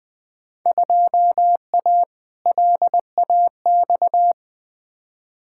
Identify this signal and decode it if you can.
Morse code, heard as a keyed tone, about 21 wpm, 700 Hz. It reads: 2A LAX
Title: Field Day Exchanges - 20wpm